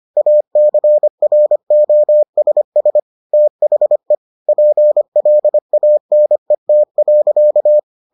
Morse code, heard as a keyed tone, about 25 wpm, 600 Hz. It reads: ACROSS THE PLANET.